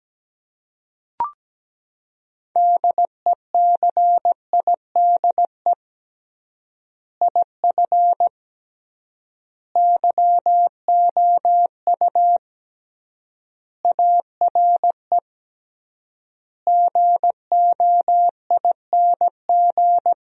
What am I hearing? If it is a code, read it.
Morse code, heard as a keyed tone, about 17 wpm, 700 Hz. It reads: DECIDE IF YOU ARE GOING
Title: 3x Word-Spacing - Sentences from Top 500 Words - 17wpm